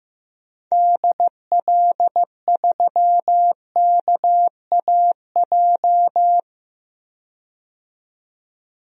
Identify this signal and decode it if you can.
Morse code, heard as a keyed tone, about 15 wpm, 700 Hz. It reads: DL3KAJ
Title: DXCC-100 Call Signs - 6 Characters - 15wpm